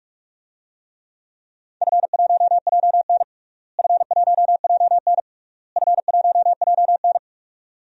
Morse code, heard as a keyed tone, about 46 wpm, 700 Hz. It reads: F1JN F1JN F1JN
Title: DXCC-100 Call Signs - 4 Characters - Repeated 3x - 45wpm